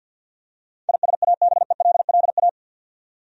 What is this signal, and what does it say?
Morse code, heard as a keyed tone, about 51 wpm, 700 Hz. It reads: ISABELLA
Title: USA Common Names - 50wpm